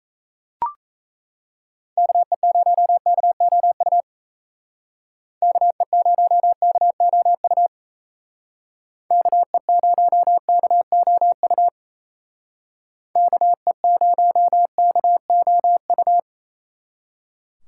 Morse code, heard as a keyed tone, about 34 wpm, 700 Hz. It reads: KE0KOU KE0KOU KE0KOU KE0KOU
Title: Speed Racing - USA 2x3 Call Signs 28wpm